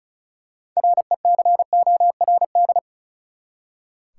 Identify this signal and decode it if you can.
Morse code, heard as a keyed tone, about 34 wpm, 700 Hz. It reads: RECORD
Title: Six Letter Words - 35wpm